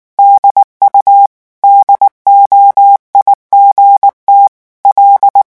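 Morse code, heard as a keyed tone, about 19 wpm, 800 Hz. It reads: DU DOIGT L
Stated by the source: French Morse revision 8 drill of random five-letter groups at 18 words per minute.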